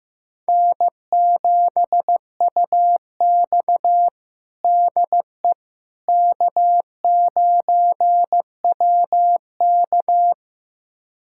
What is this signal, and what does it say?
Morse code, heard as a keyed tone, about 16 wpm, 700 Hz. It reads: N7UX DE K9WK
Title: USA Callsign DE Callsign - 4 Characters - 15wpm